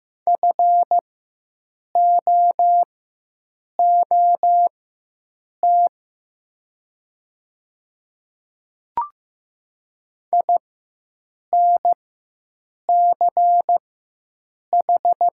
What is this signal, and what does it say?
Morse code, heard as a keyed tone, about 15 wpm, spaced as for 7 wpm, 700 Hz. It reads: FOOT INCH
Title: 4x Character Spacing - Top 500 Words - 15wpm